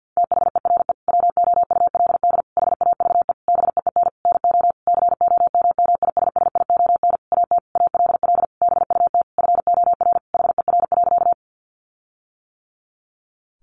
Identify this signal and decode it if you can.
Morse code, heard as a keyed tone, about 50 wpm, 700 Hz. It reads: THERE WOULD HAVE BEEN NO COMMISSION AT ALL BUT FOR HER.
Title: Sentences from Top 1400 Words 50wpm